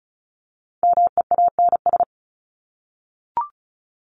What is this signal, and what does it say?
Morse code, heard as a keyed tone, about 34 wpm, 700 Hz. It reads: MEANS
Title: ES:S Suffix - Top 100 Words with Base Words 35wpm